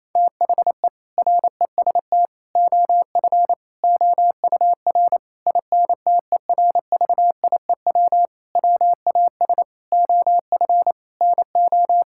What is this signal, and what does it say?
Morse code, heard as a keyed tone, about 28 wpm, 700 Hz. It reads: THE REST OF OUR INTERVIEW WAS OF NO
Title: Sentences from Top 1600 Words 28wpm